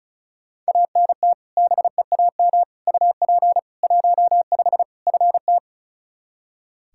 Morse code, heard as a keyed tone, about 36 wpm, 700 Hz. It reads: ANT BEAM UP 15 FT
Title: Practice QSO - Exchange 2 - Info - 35wpm